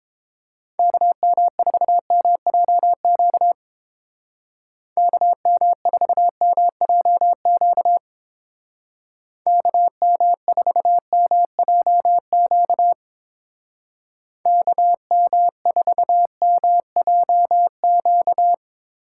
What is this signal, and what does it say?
Morse code, heard as a keyed tone, about 27 wpm, 700 Hz. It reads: KM4MJQ KM4MJQ KM4MJQ KM4MJQ
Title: Speed Racing - USA 2x3 Call Signs 22wpm